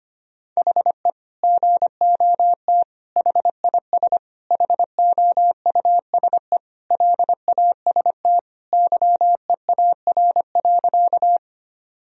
Morse code, heard as a keyed tone, about 25 wpm, 700 Hz. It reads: HE GOT HIS HOUSE LAST YEAR.
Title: Sentences from Top 1000 Words - Review Part 1 - 25wpm